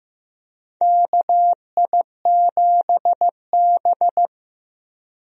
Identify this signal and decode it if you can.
Morse code, heard as a keyed tone, about 15 wpm, 700 Hz. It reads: KI7B
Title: USA Call Signs - 4 Characters - 15wpm